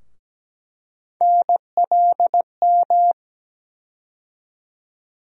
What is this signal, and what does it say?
Morse code, heard as a keyed tone, about 17 wpm, 700 Hz. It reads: NLM
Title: USA Call Sign Postfixes - 3 Letters 17wpm